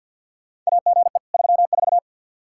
Morse code, heard as a keyed tone, about 47 wpm, 700 Hz. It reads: AGE 34
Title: Practice QSO - Exchange 3 - More Info - 50wpm